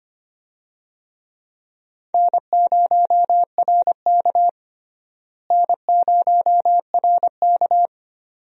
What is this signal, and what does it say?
Morse code, heard as a keyed tone, about 25 wpm, 700 Hz. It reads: N0RK N0RK
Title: USA Call Signs - 4 Characters - Repeated 2x - 25wpm